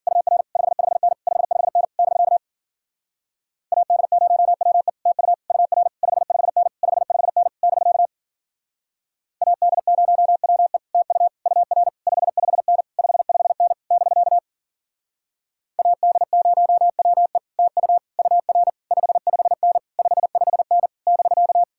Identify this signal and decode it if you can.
Morse code, heard as a keyed tone, about 47 wpm, 700 Hz. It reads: UR 55N 55N <BK> AD0WE TU UR 55N 55N <BK> AD0WE TU UR 55N 55N <BK> AD0WE TU UR 55N 55N <BK>
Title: Speed-Racing Common QSO Phrases Encore - 40wpm